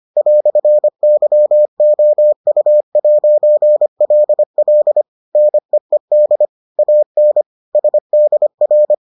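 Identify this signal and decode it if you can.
Morse code, heard as a keyed tone, about 25 wpm, 600 Hz. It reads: "YOU'LL NEED AN SDR
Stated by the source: Morse edition of the Foundations of Amateur Radio episodes.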